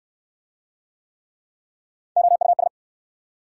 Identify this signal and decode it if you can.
Morse code, heard as a keyed tone, about 65 wpm, 700 Hz. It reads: OUS